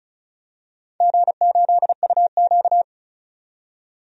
Morse code, heard as a keyed tone, about 36 wpm, 700 Hz. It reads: G8UQ
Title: DXCC-100 Call Signs - 4 Characters - 35wpm